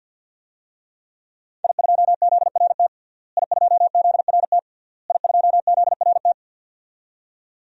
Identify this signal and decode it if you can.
Morse code, heard as a keyed tone, about 52 wpm, 700 Hz. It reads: I2ZRT I2ZRT I2ZRT
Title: DXCC-100 Call Signs - 5 Characters - Repeated 3x - 50wpm